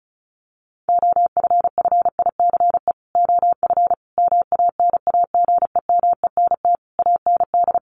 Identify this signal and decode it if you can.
Morse code, heard as a keyed tone, about 34 wpm, 700 Hz. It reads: OFFICE OF MANAGEMENT AND
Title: Sets of 5 Words Encore2 35wpm